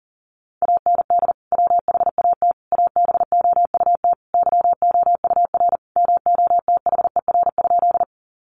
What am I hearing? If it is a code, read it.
Morse code, heard as a keyed tone, about 40 wpm, 700 Hz. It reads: AND WHAT ABOUT YOUR MOTHER?
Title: Sentences from Top 200 Words 40wpm